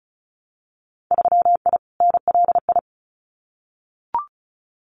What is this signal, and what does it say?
Morse code, heard as a keyed tone, about 36 wpm, 700 Hz. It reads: 3I NLI
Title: Winter Field Day Exchanges 35wpm